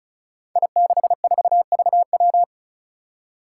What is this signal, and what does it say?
Morse code, heard as a keyed tone, about 34 wpm, 700 Hz. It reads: I64VW
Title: Five Challenging Letter-Number Combinations 35wpm